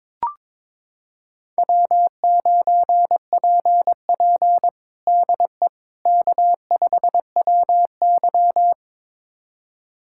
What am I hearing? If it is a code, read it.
Morse code, heard as a keyed tone, about 22 wpm, 700 Hz. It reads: W9PP DE K5WY
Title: USA Callsign DE Callsign - 4 Characters - 22wpm